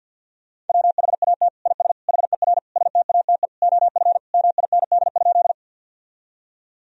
Morse code, heard as a keyed tone, about 52 wpm, 700 Hz. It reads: WHAT IS HER STATE OF MIND?
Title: Sentences from Top 1000 Words - Review Part 2 - 50wpm